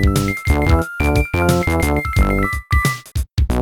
There is no code.